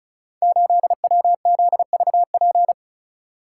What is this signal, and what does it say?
Morse code, heard as a keyed tone, about 36 wpm, 700 Hz. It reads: 8WZVP
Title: Code Groups - 5 Letter:Numbers - 35wpm